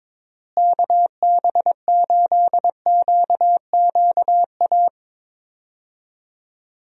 Morse code, heard as a keyed tone, about 22 wpm, 700 Hz. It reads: KB8QQA
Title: USA Call Signs - 6 Characters - Repeated 2x - 22wpm